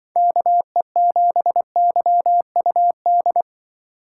Morse code, heard as a keyed tone, about 24 wpm, 700 Hz. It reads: KE7YUD
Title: Speed Racing - USA 2x3 Call Signs 20wpm